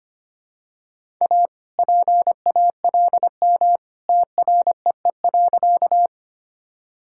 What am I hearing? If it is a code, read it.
Morse code, heard as a keyed tone, about 25 wpm, 700 Hz. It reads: A PALM TREE.
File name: practice / 25wpm / All-Age Appropriate Jokes - 25wpm